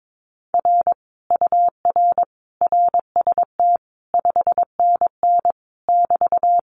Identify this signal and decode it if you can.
Morse code, heard as a keyed tone, about 22 wpm, 700 Hz. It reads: R UR RST 5NN <BT>
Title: Common QSO Phrases Encore 22wpm